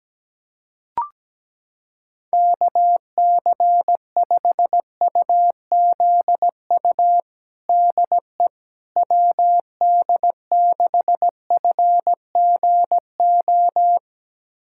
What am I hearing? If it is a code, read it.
Morse code, heard as a keyed tone, about 17 wpm, 700 Hz. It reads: KC5UZU DE WD6FGO
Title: USA Callsign DE Callsign 17wpm